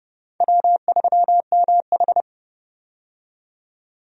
Morse code, heard as a keyed tone, about 30 wpm, 700 Hz. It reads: W3MH
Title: USA Call Signs - 4 Characters - 30wpm